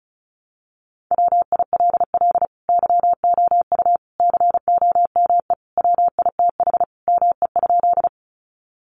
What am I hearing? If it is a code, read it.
Morse code, heard as a keyed tone, about 34 wpm, 700 Hz. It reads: WILL YOU COME WITH ME?
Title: Sentences from Top 200 Words 35wpm